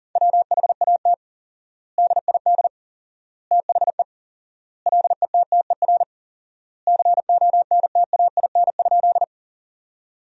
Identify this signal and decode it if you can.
Morse code, heard as a keyed tone, about 40 wpm, 700 Hz. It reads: WHAT DID THE LETTER CONTAIN?
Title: 4x Word-Spacing - Sentences from Top 500 Words - 40wpm